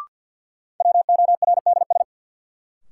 Morse code, heard as a keyed tone, about 53 wpm, 700 Hz. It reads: WORDS
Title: Sets of 3 Words - New Word Drill - 50wpm